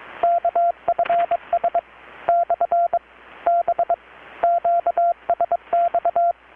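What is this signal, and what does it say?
Morse code, heard as a keyed tone, about 22 wpm, 690 Hz, with radio noise. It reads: KFS / B QSX
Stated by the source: Morse from KPH's Night of Nights XX maritime radiotelegraphy event, received in Chiapas, Mexico about 3,800 km away on a Kenwood R-600 with a grounded 10 m long-wire antenna.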